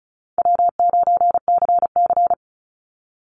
Morse code, heard as a keyed tone, about 34 wpm, 700 Hz. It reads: W9CC
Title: Easy Call Signs Repeated 3x - 35wpm